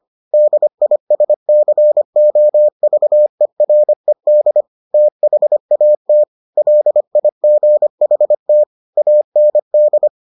Morse code, heard as a keyed tone, about 25 wpm, 600 Hz. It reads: DISCOVERED THAT LIGHT AND